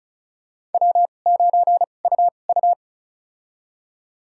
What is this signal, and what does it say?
Morse code, heard as a keyed tone, about 34 wpm, spaced as for 25 wpm, 700 Hz. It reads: W9UU
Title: USA Call Signs - 4 Characters - 2x Character Spacing - 35wpm